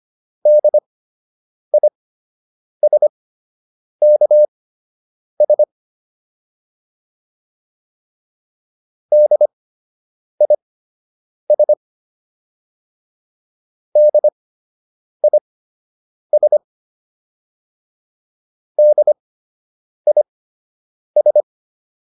Morse code, heard as a keyed tone, about 25 wpm, spaced as for 8 wpm, 600 Hz. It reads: DISKS DIS DIS DIS